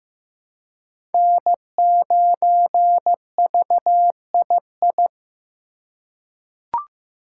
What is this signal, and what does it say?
Morse code, heard as a keyed tone, about 15 wpm, 700 Hz. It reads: N9VII
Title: USA Call Signs - 5 Characters - 4x Character Spacing - 15wpm